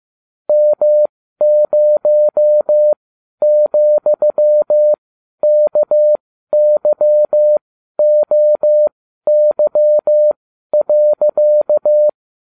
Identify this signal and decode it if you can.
Morse code, heard as a keyed tone, about 15 wpm, spaced as for 12 wpm, 600 Hz. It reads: M0,KYOY.